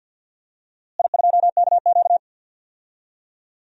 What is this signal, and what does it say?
Morse code, heard as a keyed tone, about 46 wpm, 700 Hz. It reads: I2KQ
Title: DXCC-100 Call Signs - 4 Characters - Repeated 3x - 50wpm